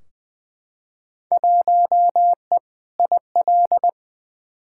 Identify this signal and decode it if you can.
Morse code, heard as a keyed tone, about 20 wpm, 700 Hz. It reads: 1E IL
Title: Field Day Exchanges - 20wpm